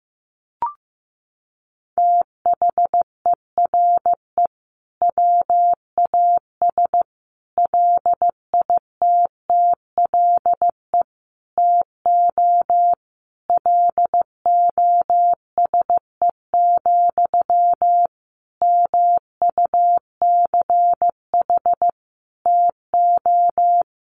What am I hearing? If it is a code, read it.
Morse code, heard as a keyed tone, about 15 wpm, 700 Hz. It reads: THERE WAS LITTLE TO LOSE, MUCH TO
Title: Sentences from Top 1100 Words 15wpm